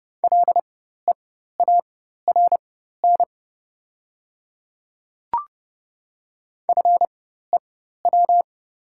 Morse code, heard as a keyed tone, about 30 wpm, spaced as for 14 wpm, 700 Hz. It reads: LEARN FEW
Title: 4x Character Spacing - Top 300 Words - 30wpm